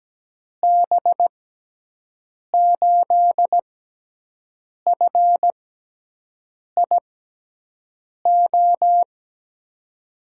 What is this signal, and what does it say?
Morse code, heard as a keyed tone, about 17 wpm, spaced as for 6 wpm, 700 Hz. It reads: B8FIO
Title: USA Call Signs - 6 Characters - 6x Character Spacing - 17wpm